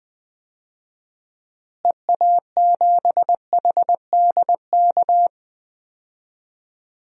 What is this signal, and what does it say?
Morse code, heard as a keyed tone, about 20 wpm, 700 Hz. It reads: EA7HDK
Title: DXCC-100 Call Signs - 6 Characters - Repeated 3x - 20wpm